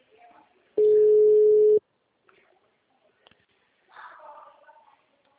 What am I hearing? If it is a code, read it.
Ringback tone (Europe, 425 Hz), with speech.
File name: out- -2004- - - .76591